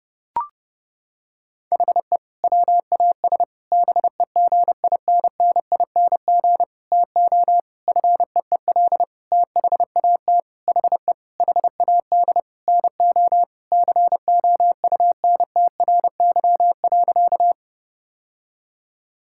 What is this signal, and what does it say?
Morse code, heard as a keyed tone, about 30 wpm, 700 Hz. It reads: HE WAS BEGINNING TO FEEL THAT HE HAD NO COUNTRY.
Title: Sentences from Top 1000 Words - Review Part 5 - 30wpm